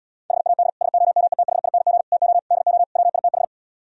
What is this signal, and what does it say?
Morse code, heard as a keyed tone, about 77 wpm, 690 Hz. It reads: HAD SOMETHING TO DO WITH